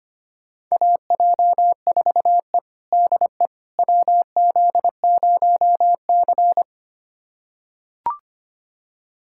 Morse code, heard as a keyed tone, about 25 wpm, 700 Hz. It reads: AJ4E DE WZ0C
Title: USA Callsign DE Callsign - 4 Characters - 25wpm